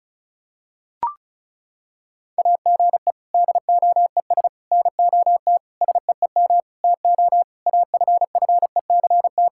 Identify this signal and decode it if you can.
Morse code, heard as a keyed tone, about 36 wpm, 700 Hz. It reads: AGE DOES NOT SEEM TO AFFECT
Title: Sentences from Top 1000 Words - Review Part 6 - 35wpm